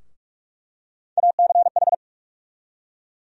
Morse code, heard as a keyed tone, about 45 wpm, 700 Hz. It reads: ACH
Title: USA Call Sign Postfixes - 3 Letters 45wpm